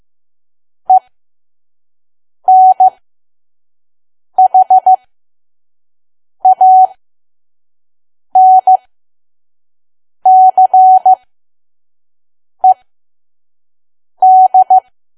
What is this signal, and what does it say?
Morse code, heard as a keyed tone, about 15 wpm, spaced as for 5 wpm, 750 Hz. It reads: ENHANCED
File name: Archive / 5 WPM files / PM